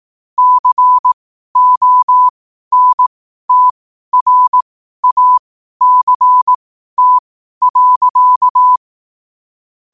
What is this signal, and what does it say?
Morse code, heard as a keyed tone, about 18 wpm, spaced as for 12 wpm, 1000 Hz. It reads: CONTRACT.
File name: qotdfiles / 12wpm